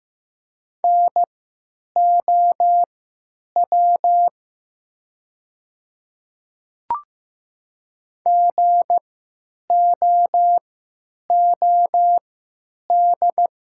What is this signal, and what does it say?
Morse code, heard as a keyed tone, about 15 wpm, spaced as for 9 wpm, 700 Hz. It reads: NOW GOOD
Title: 3x Character Spacing - Top 300 Words - 15wpm